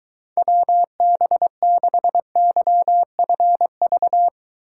Morse code, heard as a keyed tone, about 23 wpm, 700 Hz. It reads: WB6YFV